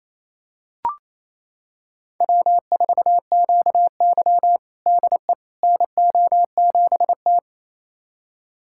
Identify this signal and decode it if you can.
Morse code, heard as a keyed tone, about 28 wpm, 700 Hz. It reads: W4QY DE NO7T
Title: USA Callsign DE Callsign - 4 Characters - 28wpm